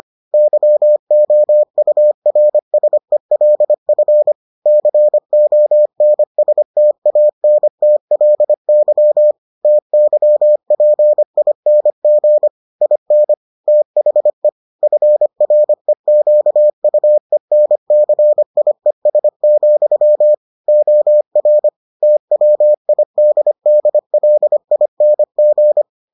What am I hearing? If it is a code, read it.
Morse code, heard as a keyed tone, about 25 wpm, 600 Hz. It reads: YOURSELF CONSTANTLY TYPING IN THE FREQUENCIES, OR TWIDDLING